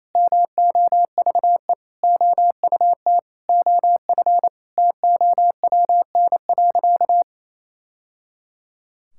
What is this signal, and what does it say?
Morse code, heard as a keyed tone, about 27 wpm, 700 Hz. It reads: MOVE OUT OF TOWN.
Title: Sentences from Top 1000 Words - Review Part 6 - 28wpm